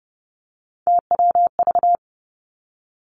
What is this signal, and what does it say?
Morse code, heard as a keyed tone, about 30 wpm, 700 Hz. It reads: TWV
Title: Three Challenging Letter-Number Combinations 30wpm